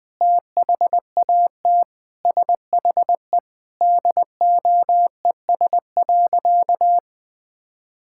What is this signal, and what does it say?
Morse code, heard as a keyed tone, about 19 wpm, 700 Hz. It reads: THAT SHE DOES.
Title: Sentences with 5-Word Phrases - 20wpm